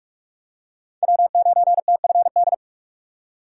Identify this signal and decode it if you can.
Morse code, heard as a keyed tone, about 42 wpm, 700 Hz. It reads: W9TFD